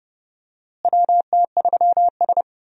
Morse code, heard as a keyed tone, about 31 wpm, 700 Hz. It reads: WT3S
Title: USA Call Signs - 4 Characters - Repeated 3x - 30wpm